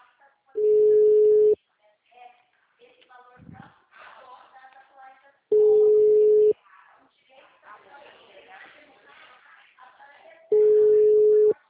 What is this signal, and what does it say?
Ringback tone (Europe, 425 Hz), with speech.
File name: out- -2011- - - .14388